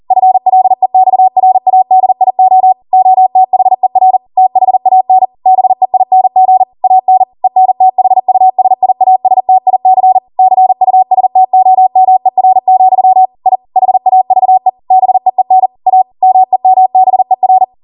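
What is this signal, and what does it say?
Morse code, heard as a keyed tone, about 40 wpm, 750 Hz. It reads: FLEXRADIO OTHER THAN BEING AN ENTHUSIASTIC CUSTOMER, I HAVE BEEN A MEMBER
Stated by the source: Code practice at 40 wpm.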